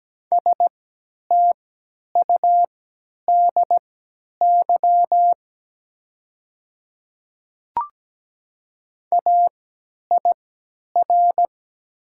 Morse code, heard as a keyed tone, about 17 wpm, spaced as for 10 wpm, 700 Hz. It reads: STUDY AIR
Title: 3x Character Spacing - Top 300 Words - 17wpm